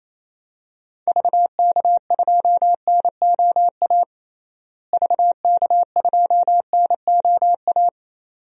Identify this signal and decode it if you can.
Morse code, heard as a keyed tone, about 27 wpm, 700 Hz. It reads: VK2NOA VK2NOA
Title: DXCC-100 Call Signs - 6 Characters - Repeated 2x - 28wpm